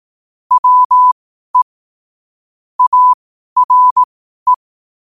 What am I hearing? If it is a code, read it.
Morse code, heard as a keyed tone, about 18 wpm, spaced as for 12 wpm, 1000 Hz. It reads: WE ARE
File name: qotdfiles / 12wpm